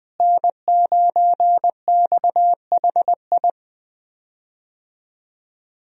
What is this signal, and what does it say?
Morse code, heard as a keyed tone, about 20 wpm, 700 Hz. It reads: N9XHI
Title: USA Call Signs - 5 Characters - Repeated 3x - 20wpm